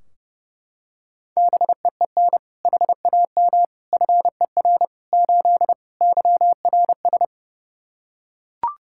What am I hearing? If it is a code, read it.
Morse code, heard as a keyed tone, about 30 wpm, 700 Hz. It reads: BEEN HAM FER 8 YRS
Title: Practice QSO - Exchange 3 - More Info - 30wpm